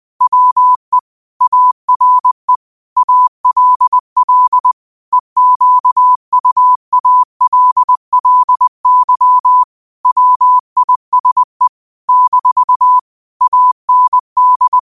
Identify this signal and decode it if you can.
Morse code, heard as a keyed tone, about 20 wpm, 1000 Hz. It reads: WE ARE ALL EQUALLY WISE - AND
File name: qotdfiles / 20wpm